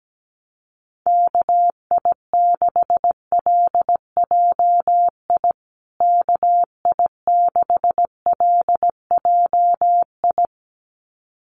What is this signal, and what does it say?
Morse code, heard as a keyed tone, about 17 wpm, 700 Hz. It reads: KI6LJI KI6LJI
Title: USA Call Signs Repeated 2x 17wpm